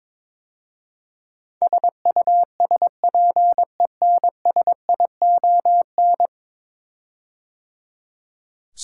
Morse code, heard as a keyed tone, about 22 wpm, 700 Hz. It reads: SUSPENSION
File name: practice / 22wpm / 10 Letter Words - First 3 Letters Given - 22wpm